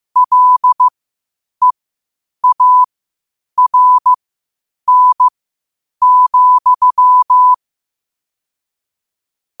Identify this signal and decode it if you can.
Morse code, heard as a keyed tone, about 15 wpm, spaced as for 8 wpm, 1000 Hz. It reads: LEARN,
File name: 8wpm